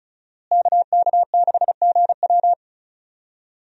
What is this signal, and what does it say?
Morse code, heard as a keyed tone, about 36 wpm, 700 Hz. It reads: KK6GW
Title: USA Call Signs - 5 Characters - Repeated 3x - 35wpm